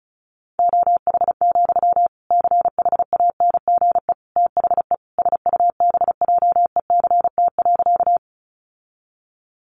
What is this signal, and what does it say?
Morse code, heard as a keyed tone, about 34 wpm, 700 Hz. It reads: OH, CHANGE THE SUBJECT.
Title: Sentences from Top 800 Words 35wpm